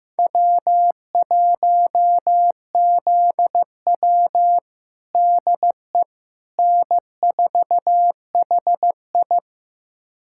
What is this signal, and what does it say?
Morse code, heard as a keyed tone, about 15 wpm, 700 Hz. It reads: W1ZW DE N4HI
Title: USA Callsign DE Callsign - 4 Characters - 15wpm